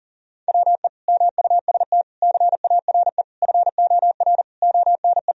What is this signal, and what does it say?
Morse code, heard as a keyed tone, about 40 wpm, 700 Hz. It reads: WE MUST CARE FOR ONE
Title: Sentences from Top 1000 Words - Review Part 1 - 40wpm